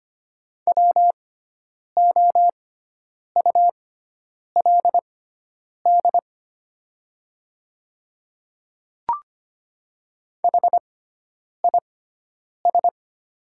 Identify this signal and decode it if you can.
Morse code, heard as a keyed tone, about 25 wpm, spaced as for 9 wpm, 700 Hz. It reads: WOULD HIS
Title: 6x Character Spacing - Top 300 Words - 25wpm